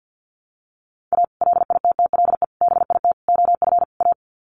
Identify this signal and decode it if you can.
Morse code, heard as a keyed tone, about 50 wpm, 700 Hz. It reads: A LITTLE BIT OF A
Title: Sets of 5 Words 50wpm